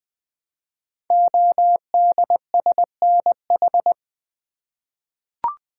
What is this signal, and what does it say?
Morse code, heard as a keyed tone, about 20 wpm, 700 Hz. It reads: ODSNH
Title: Code Groups - 5 Letter:Numbers - 20wpm